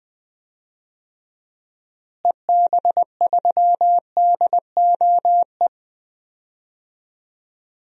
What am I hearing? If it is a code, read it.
Morse code, heard as a keyed tone, about 21 wpm, 700 Hz. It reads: EB3DOE
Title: DXCC-100 Call Signs - 6 Characters - Repeated 3x - 20wpm